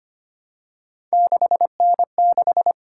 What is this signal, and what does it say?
Morse code, heard as a keyed tone, about 25 wpm, 700 Hz. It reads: 6N6
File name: Three Letter-Number Combinations - 25wpm